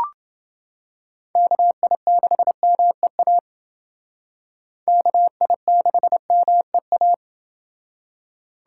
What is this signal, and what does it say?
Morse code, heard as a keyed tone, about 28 wpm, 700 Hz. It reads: KI6MEA KI6MEA